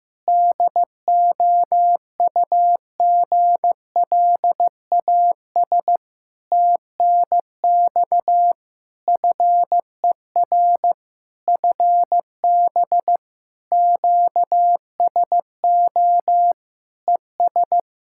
Morse code, heard as a keyed tone, about 15 wpm, 700 Hz. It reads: DOUGLAS TNX FER FB QSO ES
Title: Practice QSO - Exchange 4 - Ending - 15wpm